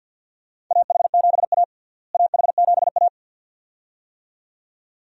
Morse code, heard as a keyed tone, about 48 wpm, 700 Hz. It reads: AH7A AH7A
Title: USA Call Signs - 4 Characters - Repeated 2x - 50wpm